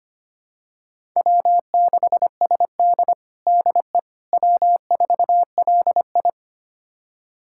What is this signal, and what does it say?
Morse code, heard as a keyed tone, about 25 wpm, 700 Hz. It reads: W6SD DE W4LI
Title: USA Callsign DE Callsign - 4 Characters - 25wpm